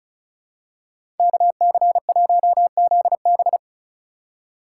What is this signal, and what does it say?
Morse code, heard as a keyed tone, about 36 wpm, 700 Hz. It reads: KC1ZB
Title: USA Call Signs - 4x Character Spacing - 35wpm